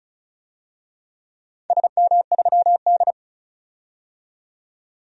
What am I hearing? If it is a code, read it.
Morse code, heard as a keyed tone, about 36 wpm, 700 Hz. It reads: SM3D
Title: DXCC-100 Call Signs - 4 Characters - 35wpm